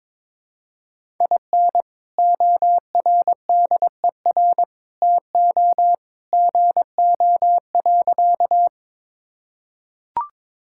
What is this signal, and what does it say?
Morse code, heard as a keyed tone, about 21 wpm, 700 Hz. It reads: IN ORDER TO GO.
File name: practice / 22wpm / Sentences with 2-Word Phrases - 22wpm